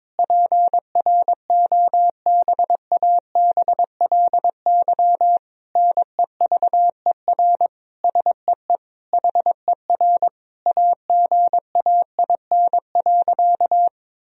Morse code, heard as a keyed tone, about 21 wpm, 700 Hz. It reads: PROBABLY NEVER SEE HER AGAIN.
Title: Sentences from Top 1000 Words - Review Part 4 - 22wpm